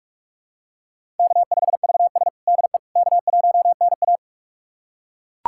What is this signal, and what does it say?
Morse code, heard as a keyed tone, about 41 wpm, 700 Hz. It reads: K5VS DE K1NA
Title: USA Callsign DE Callsign - 4 Characters - 45wpm